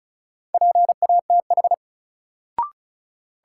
Morse code, heard as a keyed tone, about 36 wpm, 700 Hz. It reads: PATH
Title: Sentences from Top 1400 Words - New Word Drill - 35wpm